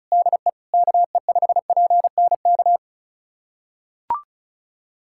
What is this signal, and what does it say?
Morse code, heard as a keyed tone, about 36 wpm, 690 Hz. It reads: DE KE5PNK
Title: USA Callsign DE Callsign - 6 Characters - 35wpm